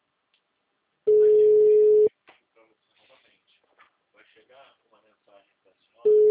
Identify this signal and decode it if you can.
Ringback tone (Europe, 425 Hz), with speech.